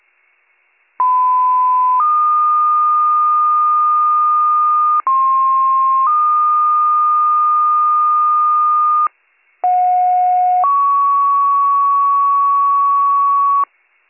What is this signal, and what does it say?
Two-tone page, heard as a one-second tone, then a three-second tone, about 1010 then 1220 Hz, about 1060 then 1220 Hz, about 730 then 1120 Hz. It is heard from 1.0 to 13.6 s.